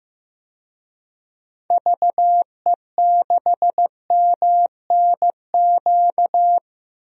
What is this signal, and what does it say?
Morse code, heard as a keyed tone, about 16 wpm, 700 Hz. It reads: VE6MNQ
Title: DXCC-100 Call Signs - 6 Characters - 15wpm